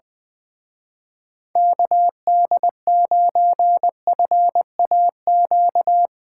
Morse code, heard as a keyed tone, about 22 wpm, 700 Hz. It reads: KD9FAQ
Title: Speed Racing - USA Call Signs 20wpm